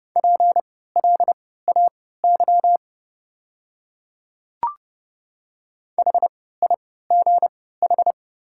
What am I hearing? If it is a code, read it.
Morse code, heard as a keyed tone, about 29 wpm, spaced as for 17 wpm, 690 Hz. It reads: PLAY HIGH
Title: 3x Character Spacing - Top 200 Words - 30wpm